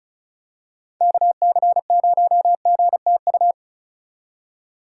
Morse code, heard as a keyed tone, about 36 wpm, 700 Hz. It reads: KC0GTU
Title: USA Call Signs - 6 Characters - 2x Character Spacing - 35wpm